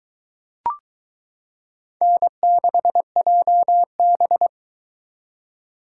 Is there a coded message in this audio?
Yes: Morse code from 2.0 to 4.5 s, heard as a keyed tone, about 22 wpm, 700 Hz. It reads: N6JB